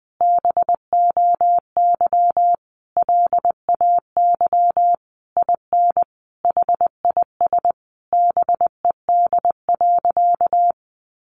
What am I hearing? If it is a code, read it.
Morse code, heard as a keyed tone, about 20 wpm, 700 Hz. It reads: BOY LAY IN HIS BED.
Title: Sentences from Top 500 Words 20wpm